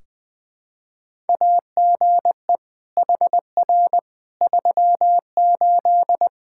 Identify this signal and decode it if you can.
Morse code, heard as a keyed tone, about 20 wpm, 700 Hz. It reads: AGE HR 38
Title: Practice QSO - Exchange 3 - More Info - 20wpm